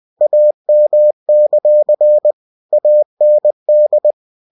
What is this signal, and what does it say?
Morse code, heard as a keyed tone, about 20 wpm, 600 Hz. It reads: AM; AND
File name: cw-books / winnie-the-pooh / winnie-20wpm-page-0079